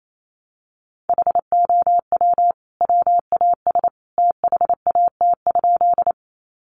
Morse code, heard as a keyed tone, about 28 wpm, 700 Hz. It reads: HOW WAS THAT?
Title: Sentences from Top 100 Words 28wpm